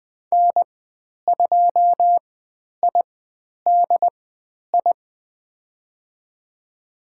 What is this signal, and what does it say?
Morse code, heard as a keyed tone, about 20 wpm, spaced as for 10 wpm, 700 Hz. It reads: N2IDI
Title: 109 Lesson 18g - Letter m - Call Signs 20wpm